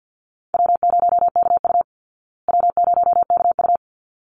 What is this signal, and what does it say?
Morse code, heard as a keyed tone, about 49 wpm, 700 Hz. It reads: W0KU W0KU
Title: Easy Call Signs Repeated 2x - 50wpm